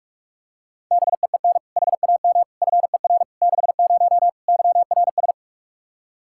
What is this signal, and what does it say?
Morse code, heard as a keyed tone, about 46 wpm, 700 Hz. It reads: BEEN HAM FER 60 YRS
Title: Practice QSO - Exchange 3 - More Info - 45wpm